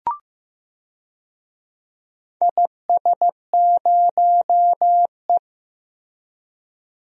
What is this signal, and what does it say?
Morse code, heard as a keyed tone, about 15 wpm, 700 Hz. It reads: IS0E
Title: DXCC-100 Call Signs - 4 Characters - Repeated 2x - 15wpm